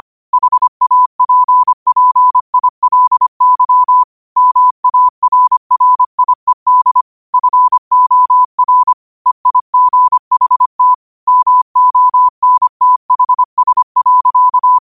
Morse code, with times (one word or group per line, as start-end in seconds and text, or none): 0.3-4.0 HAPPILY
4.4-7.0 MARRIED
7.3-8.9 FOR
9.3-10.9 EIGHT
11.3-14.8 MONTHS.